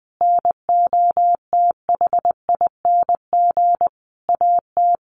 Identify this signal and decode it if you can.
Morse code, heard as a keyed tone, about 20 wpm, 700 Hz. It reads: NOTHING AT
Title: Sentences from Top 1300 Words 20wpm